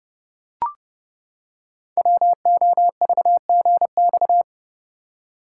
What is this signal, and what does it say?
Morse code, heard as a keyed tone, about 30 wpm, 700 Hz. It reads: WOVGX
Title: Code Groups - 5 Letters - 30wpm